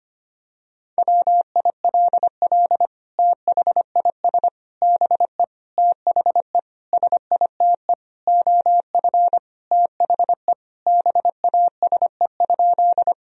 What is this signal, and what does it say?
Morse code, heard as a keyed tone, about 25 wpm, 700 Hz. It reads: WILL THIS BE THE SITE OF THE BASE?
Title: Sentences from Top 1000 Words - Review Part 1 - 25wpm